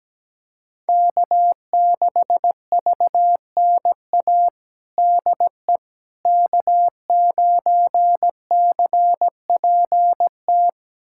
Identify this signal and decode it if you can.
Morse code, heard as a keyed tone, about 17 wpm, 700 Hz. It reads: K6VNA DE K9CPT
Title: USA Callsign DE Callsign - 5 Characters - 17wpm